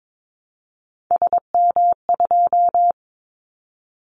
Morse code, heard as a keyed tone, about 22 wpm, 700 Hz. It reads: SM2
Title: Callsign Prefixes for Top 25 DXCC - Repeated 3x - 22wpm